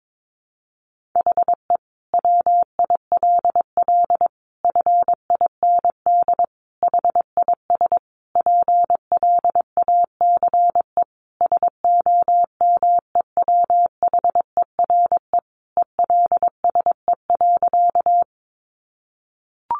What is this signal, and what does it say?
Morse code, heard as a keyed tone, about 22 wpm, 700 Hz. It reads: HE WILL FIND HIS PLACE SOMEWHERE ELSE.
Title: Sentences from Top 1400 Words 22wpm